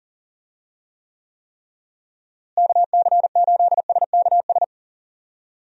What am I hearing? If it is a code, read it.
Morse code, heard as a keyed tone, about 40 wpm, 700 Hz. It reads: KC8SKS
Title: USA Call Signs - 6 Characters - Repeated 3x - 40wpm